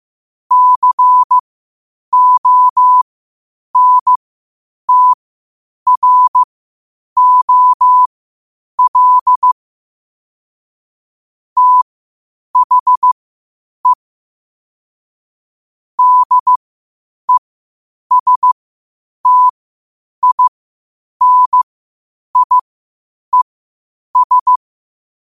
Morse code, heard as a keyed tone, about 15 wpm, spaced as for 8 wpm, 1000 Hz. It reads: CONTROL THE DESTINIES
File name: qotdfiles / 8wpm